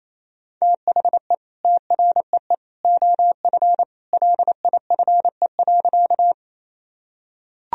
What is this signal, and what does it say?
Morse code, heard as a keyed tone, about 28 wpm, 700 Hz. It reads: THE TREE OF LIFE.
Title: Sentences from Top 1000 Words - Review Part 1 - 28wpm